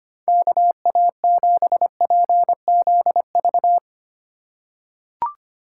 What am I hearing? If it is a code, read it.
Morse code, heard as a keyed tone, about 25 wpm, 700 Hz. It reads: KA7PZV
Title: USA Call Signs - 6 Characters - 4x Character Spacing - 25wpm